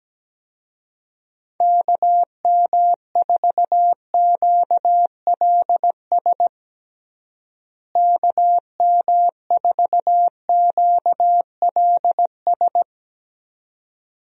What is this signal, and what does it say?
Morse code, heard as a keyed tone, about 17 wpm, 700 Hz. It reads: KM4QLS KM4QLS
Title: USA Call Signs - 6 Characters - Repeated 2x - 17wpm